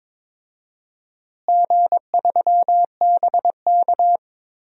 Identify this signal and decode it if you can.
Morse code, heard as a keyed tone, about 22 wpm, 700 Hz. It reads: G3BK
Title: DXCC-100 Call Signs - 4 Characters - Repeated 3x - 22wpm